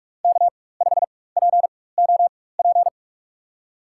Morse code, heard as a keyed tone, about 46 wpm, spaced as for 21 wpm, 700 Hz. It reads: K5POP
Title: USA Call Signs - 5 Characters - 4x Character Spacing - 45wpm